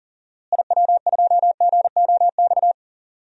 Morse code, heard as a keyed tone, about 40 wpm, 700 Hz. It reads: IW2GOX
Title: DXCC-100 Call Signs - 6 Characters - Repeated 2x - 40wpm